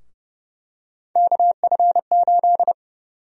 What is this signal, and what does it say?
Morse code, heard as a keyed tone, about 31 wpm, 700 Hz. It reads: KF8
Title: USA Call Sign Prefixes - 30wpm